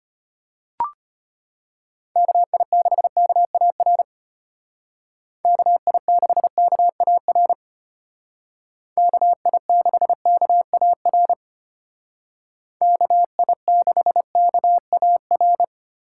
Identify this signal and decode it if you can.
Morse code, heard as a keyed tone, about 33 wpm, 700 Hz. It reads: KI6KAR KI6KAR KI6KAR KI6KAR